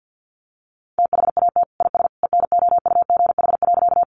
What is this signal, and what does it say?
Morse code, heard as a keyed tone, about 50 wpm, 700 Hz. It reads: THAT IS ENOUGH.
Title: Sentences from Top 400 Words 50wpm